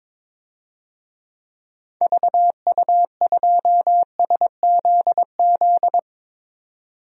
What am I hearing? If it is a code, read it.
Morse code, heard as a keyed tone, about 22 wpm, 700 Hz. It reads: VU2SZZ
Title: DXCC-100 Call Signs - 6 Characters - 22wpm